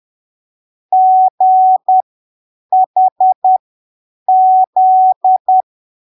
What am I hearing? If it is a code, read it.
Morse code, heard as a keyed tone, about 10 wpm, spaced as for 7 wpm, 750 Hz. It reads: GHZ
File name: arquivos / texto / texto 12 10ppm